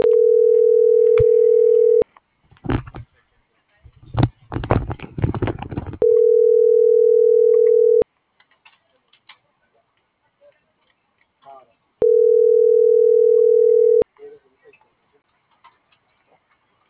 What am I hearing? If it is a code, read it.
Ringback tone (North America), 440 + 480 Hz.
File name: CENTRAL GENERAL / out laura.albarran